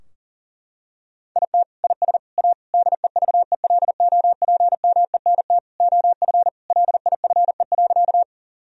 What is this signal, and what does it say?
Morse code, heard as a keyed tone, about 37 wpm, 700 Hz. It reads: IT IS A DEVELOPMENT OF LIFE.
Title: Sentences from Top 1000 Words - Review Part 4 - 40wpm